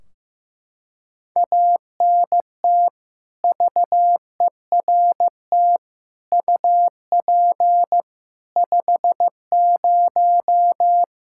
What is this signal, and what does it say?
Morse code, heard as a keyed tone, about 15 wpm, 700 Hz. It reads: ANT VERT UP 50
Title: Practice QSO - Exchange 2 - Info - 15wpm